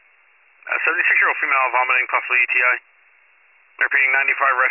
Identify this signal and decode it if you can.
Emergency Alert System (SAME) tones, heard as harsh buzzing bursts, with speech. It is heard from 1.1 to 4.3 s.